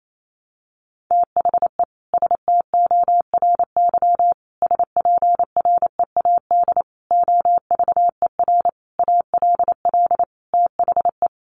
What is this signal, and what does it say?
Morse code, heard as a keyed tone, about 28 wpm, 700 Hz. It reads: THE STORY SPREAD OVER ALL THE
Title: Sentences from Top 1200 Words 28wpm